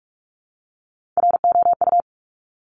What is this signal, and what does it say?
Morse code, heard as a keyed tone, about 46 wpm, 700 Hz. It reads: ROU
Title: Top Three Letter Combinations in Words 45wpm